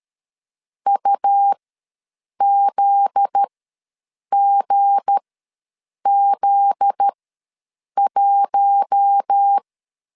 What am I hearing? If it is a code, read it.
Morse code, heard as a keyed tone, about 13 wpm, spaced as for 7 wpm, 800 Hz. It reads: UZGZ1
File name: 7 GF9Z ws 12wpm